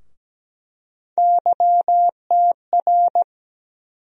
Morse code, heard as a keyed tone, about 17 wpm, 700 Hz. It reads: YTR
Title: USA Call Sign Postfixes - 3 Letters 17wpm